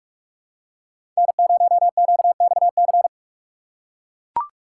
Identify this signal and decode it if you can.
Morse code, heard as a keyed tone, about 44 wpm, 690 Hz. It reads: N0QXC
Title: USA Call Signs - 5 Characters - 6x Character Spacing - 45wpm